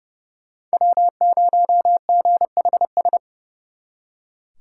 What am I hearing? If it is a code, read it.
Morse code, heard as a keyed tone, about 37 wpm, 700 Hz. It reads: W0GHS
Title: Speed Racing - USA Call Signs 30wpm